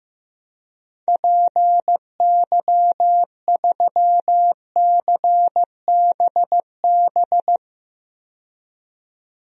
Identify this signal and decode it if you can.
Morse code, heard as a keyed tone, about 15 wpm, 700 Hz. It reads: PY3CBB
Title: DXCC-100 Call Signs - 6 Characters - 15wpm